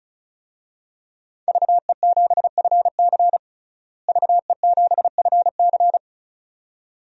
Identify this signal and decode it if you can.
Morse code, heard as a keyed tone, about 36 wpm, 700 Hz. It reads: VE7FC VE7FC
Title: DXCC-100 Call Signs - 5 Characters - Repeated 2x - 35wpm